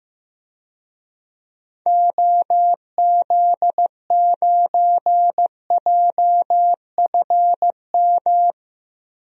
Morse code, heard as a keyed tone, about 15 wpm, 700 Hz. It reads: OZ9JFM
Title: DXCC-100 Call Signs - 6 Characters - Repeated 2x - 15wpm